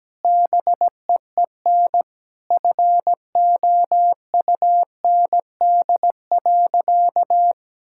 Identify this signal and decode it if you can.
Morse code, heard as a keyed tone, about 17 wpm, 700 Hz. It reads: BEEN FOUND.
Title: Sentences from Top 1600 Words 17wpm